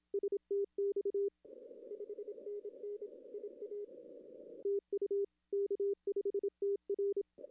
Morse code, heard as a keyed tone, about 27 wpm, 400 Hz. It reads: STX 5NN IA TU K5TR
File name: contests / arrl-160-k5tr-op-w5tn -0200